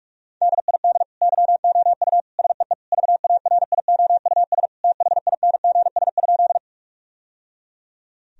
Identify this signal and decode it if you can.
Morse code, heard as a keyed tone, about 46 wpm, 700 Hz. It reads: DID YOU SEE VARIOUS THINGS?
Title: Sentences from Top 1000 Words - Review Part 4 - 45wpm